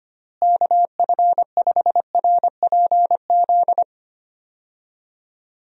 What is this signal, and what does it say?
Morse code, heard as a keyed tone, about 25 wpm, 700 Hz. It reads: KF5RPZ
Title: USA Call Signs - 6 Characters - Repeated 2x - 25wpm